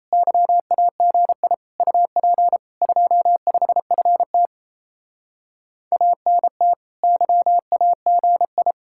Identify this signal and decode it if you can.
Morse code, heard as a keyed tone, about 35 wpm, 700 Hz. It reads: YAGI UP 25FT ANT YAGI
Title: Speed-Racing Common QSO Phrases Encore - 28wpm